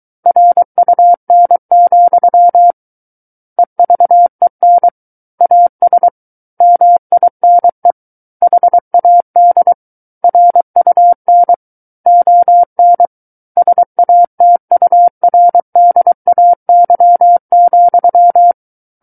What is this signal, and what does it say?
Morse code, heard as a keyed tone, about 23 wpm, 700 Hz. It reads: RUN, EVEN AS MINE HAD RUN ON SATURDAY,
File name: Book One / 14-Chapter 14 - 23 WPM